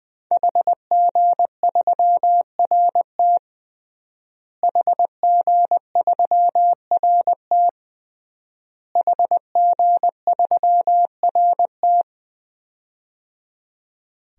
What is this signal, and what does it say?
Morse code, heard as a keyed tone, about 20 wpm, 700 Hz. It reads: HG3RT HG3RT HG3RT
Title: DXCC-100 Call Signs - 5 Characters - Repeated 3x - 20wpm